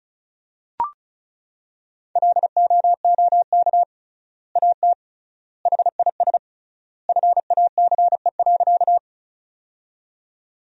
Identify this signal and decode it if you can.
Morse code, heard as a keyed tone, about 36 wpm, 700 Hz. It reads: LOOK AT HIS FACE.
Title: 3x Word-Spacing - Sentences from Top 300 Words - 35wpm